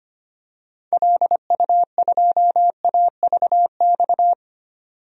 Morse code, heard as a keyed tone, about 25 wpm, 700 Hz. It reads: LU2AVX
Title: DXCC-100 Call Signs - 6 Characters - 25wpm